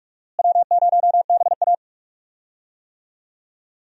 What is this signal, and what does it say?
Morse code, heard as a keyed tone, about 46 wpm, 700 Hz. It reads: W0BA
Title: USA Call Signs - 4 Characters - Repeated 2x - 45wpm